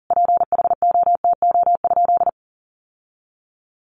Morse code, heard as a keyed tone, about 37 wpm, 700 Hz. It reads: PHOTO?
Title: Sentences from Top 1200 Words 40wpm